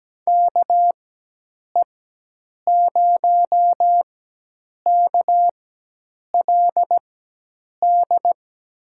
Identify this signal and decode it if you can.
Morse code, heard as a keyed tone, about 17 wpm, spaced as for 8 wpm, 700 Hz. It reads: KE0KLD
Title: USA Call Signs - 6 Characters - 4x Character Spacing - 17wpm